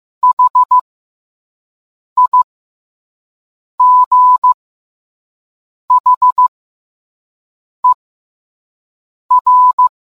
Morse code, heard as a keyed tone, about 15 wpm, spaced as for 5 wpm, 1000 Hz. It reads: HIGHER